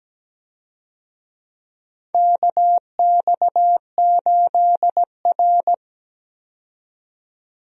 Morse code, heard as a keyed tone, about 17 wpm, 700 Hz. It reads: KX8R
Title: USA Call Signs - 4 Characters - 17wpm